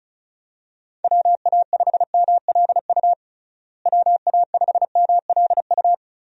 Morse code, heard as a keyed tone, about 36 wpm, 700 Hz. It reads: WA5MLU WA5MLU
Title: USA Call Signs - 6 Characters - Repeated 3x - 35wpm